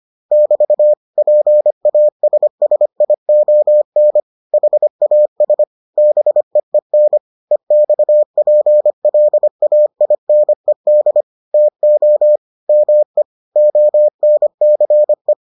Morse code, heard as a keyed tone, about 25 wpm, 600 Hz. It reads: <BT> PASSION HAS BEEN EXPLAINED TO ME ONCE